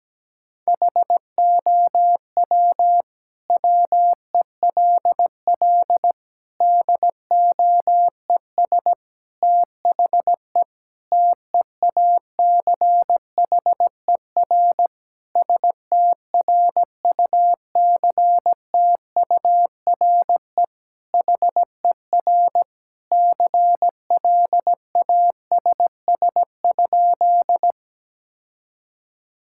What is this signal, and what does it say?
Morse code, heard as a keyed tone, about 17 wpm, 700 Hz. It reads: HOW WELL DOES THE TEACHER STRUCTURE HER CLASS?
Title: Sentences from Top 1000 Words - Review Part 6 - 17wpm